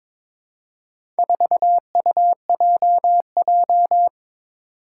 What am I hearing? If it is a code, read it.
Morse code, heard as a keyed tone, about 22 wpm, 700 Hz. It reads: 4UJJ
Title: Four Challenging Letter-Number Combinations 22wpm